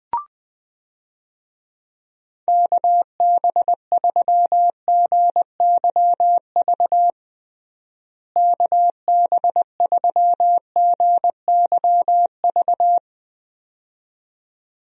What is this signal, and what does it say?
Morse code, heard as a keyed tone, about 20 wpm, 700 Hz. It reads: KB3GYV KB3GYV
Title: USA Call Signs - 6 Characters - Repeated 2x - 20wpm